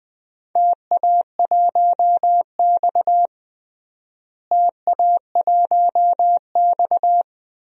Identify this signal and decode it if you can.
Morse code, heard as a keyed tone, about 20 wpm, 700 Hz. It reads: TA1X TA1X
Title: DXCC-100 Call Signs - 4 Characters - Repeated 2x - 20wpm